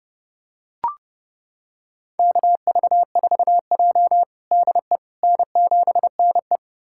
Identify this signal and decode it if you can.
Morse code, heard as a keyed tone, about 30 wpm, 700 Hz. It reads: KV4J DE N7NE
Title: USA Callsign DE Callsign - 4 Characters - 30wpm